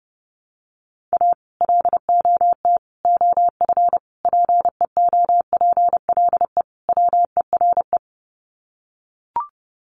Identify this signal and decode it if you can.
Morse code, heard as a keyed tone, about 30 wpm, 700 Hz. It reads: A LOT OF PEOPLE WERE
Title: Sets of 5 Words Encore2 30wpm